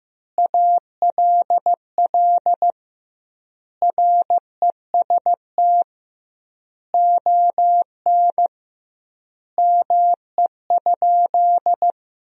Morse code, heard as a keyed tone, about 15 wpm, 700 Hz. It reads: ALL REST ON ME?
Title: 2x Word-Spacing - Sentences from Top 500 Words - 15wpm